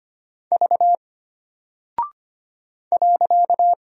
Morse code, heard as a keyed tone, about 25 wpm, 700 Hz. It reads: V .
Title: Single Character (Letter:Number:Punctuation:Prosign) Alt 25wpm